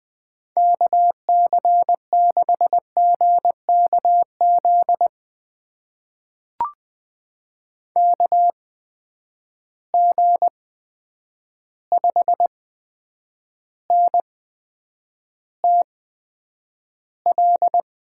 Morse code, heard as a keyed tone, about 20 wpm, spaced as for 9 wpm, 700 Hz. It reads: KC6GKZ KG5NTL
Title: USA Call Signs - 6 Characters - 8x Character Spacing - 20wpm